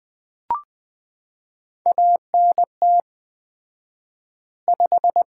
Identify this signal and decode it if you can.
Morse code, heard as a keyed tone, about 20 wpm, 700 Hz. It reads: ANT 5
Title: Common QSO Phrases - 4x Word-Spacing - 20wpm